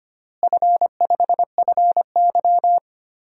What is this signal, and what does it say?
Morse code, heard as a keyed tone, about 25 wpm, 700 Hz. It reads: F5FY
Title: DXCC-100 Call Signs - 4 Characters - Repeated 2x - 25wpm